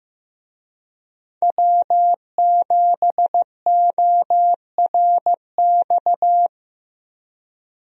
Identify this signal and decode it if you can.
Morse code, heard as a keyed tone, about 15 wpm, 700 Hz. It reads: W7ORX
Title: USA Call Signs - 5 Characters - Repeated 2x - 15wpm